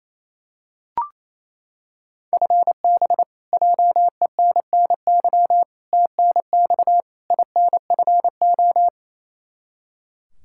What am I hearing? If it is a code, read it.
Morse code, heard as a keyed tone, about 28 wpm, 700 Hz. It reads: FB JENNY TNX INFO
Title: Practice QSO - Exchange 3 - More Info - 28wpm